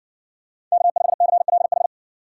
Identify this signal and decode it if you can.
Morse code, heard as a keyed tone, about 55 wpm, 700 Hz. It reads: K5CLH